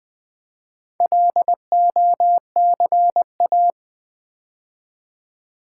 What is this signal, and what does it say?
Morse code, heard as a keyed tone, about 20 wpm, 700 Hz. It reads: LOCA
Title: Build A Word - 2 Letters At A Time - 8-Letter Words - 20wpm